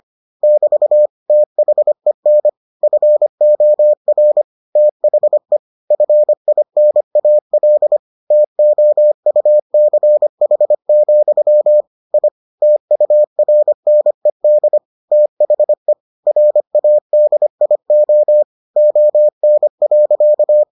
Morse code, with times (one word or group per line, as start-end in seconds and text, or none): 0.4-1.1 <BT>
1.3-2.5 THEN
2.8-4.4 FOR
4.8-5.6 THE
5.9-8.0 FINAL
8.3-11.8 TOUCH,
12.1-12.3 I
12.6-14.8 TURNED
15.1-15.9 THE
16.3-18.4 RADIO
18.8-20.6 ON.